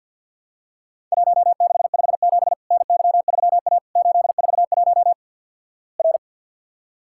Morse code, heard as a keyed tone, about 48 wpm, 700 Hz. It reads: 1657 NY3A 841 R
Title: Worked All Europe DX Contest - QTC Exchange - 50wpm